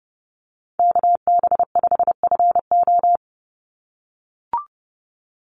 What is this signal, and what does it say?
Morse code, heard as a keyed tone, about 30 wpm, 700 Hz. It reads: KB5FO
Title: USA Call Signs 30wpm